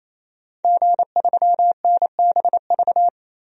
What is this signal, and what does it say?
Morse code, heard as a keyed tone, about 27 wpm, 700 Hz. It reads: G3NBV
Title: DXCC-100 Call Signs - 5 Characters - 28wpm